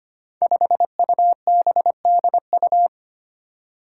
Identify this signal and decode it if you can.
Morse code, heard as a keyed tone, about 25 wpm, 700 Hz. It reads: 5UBDU
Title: Five Challenging Letter-Number Combinations 25wpm